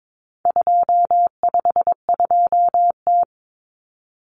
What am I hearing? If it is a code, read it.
Morse code, heard as a keyed tone, about 22 wpm, 700 Hz. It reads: 252T
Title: Common Cut Numbers - 4-Digit 22wpm